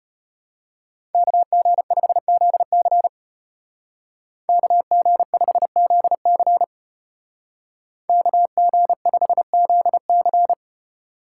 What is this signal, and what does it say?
Morse code, heard as a keyed tone, about 32 wpm, 700 Hz. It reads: KG5ZC KG5ZC KG5ZC